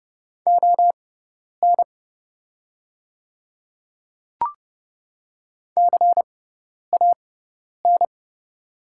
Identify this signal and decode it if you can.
Morse code, heard as a keyed tone, about 30 wpm, spaced as for 10 wpm, 700 Hz. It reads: ON CAN
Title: 6x Character Spacing - Top 100 Words - 30wpm